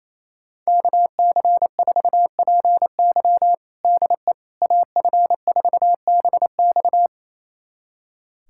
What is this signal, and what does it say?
Morse code, heard as a keyed tone, about 28 wpm, 700 Hz. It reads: KC4PY DE AF4BX
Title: USA Callsign DE Callsign - 5 Characters - 28wpm